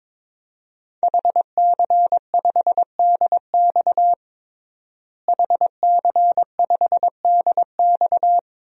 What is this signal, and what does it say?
Morse code, heard as a keyed tone, about 22 wpm, 700 Hz. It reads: HC5DX HC5DX
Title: DXCC-100 Call Signs - 5 Characters - Repeated 3x - 22wpm